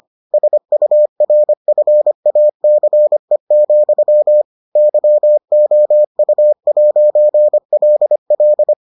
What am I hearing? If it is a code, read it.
Morse code, heard as a keyed tone, about 25 wpm, 600 Hz. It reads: SURFACE, YOU'LL